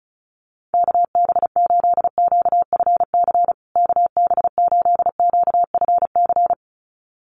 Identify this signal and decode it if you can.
Morse code, heard as a keyed tone, about 34 wpm, 700 Hz. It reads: KB8QFC KB8QFC
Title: USA Call Signs Repeated 2x 35wpm